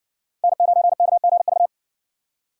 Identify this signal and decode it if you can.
Morse code, heard as a keyed tone, about 60 wpm, 700 Hz. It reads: N9KGV